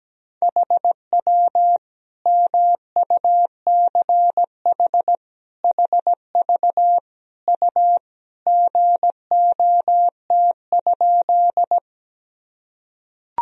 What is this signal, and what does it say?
Morse code, heard as a keyed tone, about 17 wpm, 700 Hz. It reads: HW MUCH HV U GOT?
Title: Sentences from Top 300 Words and Abbreviations 17wpm